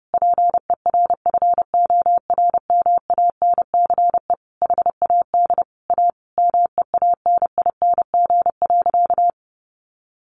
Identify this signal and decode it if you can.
Morse code, heard as a keyed tone, about 30 wpm, 700 Hz. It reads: PERFORMANCE HAD A MEANING.
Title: Sentences from Top 1400 Words 30wpm